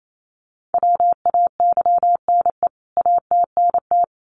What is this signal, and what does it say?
Morse code, heard as a keyed tone, about 28 wpm, 700 Hz. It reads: WAYNE ATNT
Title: CWT Exchange 28wpm